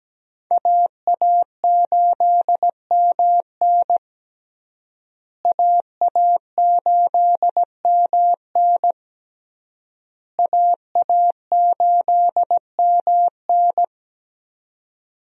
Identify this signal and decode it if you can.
Morse code, heard as a keyed tone, about 17 wpm, 700 Hz. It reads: AA8MN AA8MN AA8MN
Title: USA Call Signs - 5 Characters - Repeated 3x - 17wpm